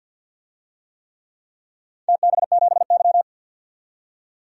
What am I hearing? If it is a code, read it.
Morse code, heard as a keyed tone, about 51 wpm, 700 Hz. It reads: TB7Y
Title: DXCC-100 Call Signs - 4 Characters - 50wpm